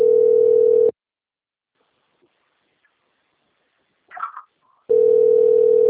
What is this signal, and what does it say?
Ringback tone (North America), 440 + 480 Hz.